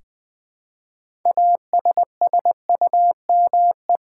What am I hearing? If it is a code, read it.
Morse code, heard as a keyed tone, about 20 wpm, 700 Hz. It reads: ASSUME
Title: Six Letter Words - 20wpm